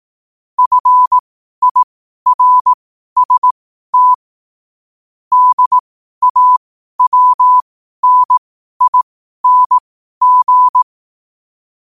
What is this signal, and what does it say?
Morse code, heard as a keyed tone, about 18 wpm, spaced as for 12 wpm, 1000 Hz. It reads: FIRST DAWNING
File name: qotdfiles / 12wpm